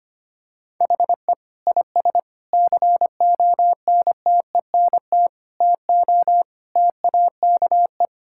Morse code, heard as a keyed tone, about 25 wpm, 700 Hz. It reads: HE IS CONTENT TO TAKE
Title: Sentences from Top 1000 Words - Review Part 6 - 25wpm